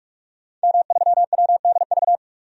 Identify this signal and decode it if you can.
Morse code, heard as a keyed tone, about 44 wpm, 690 Hz. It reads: M3WDV
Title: DXCC-100 Call Signs - 5 Characters - 45wpm